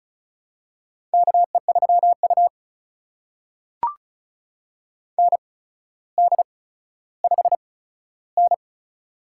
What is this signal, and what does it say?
Morse code, heard as a keyed tone, about 36 wpm, spaced as for 15 wpm, 700 Hz. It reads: KE3U ND5N
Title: USA Call Signs - 4 Characters - 8x Character Spacing - 35wpm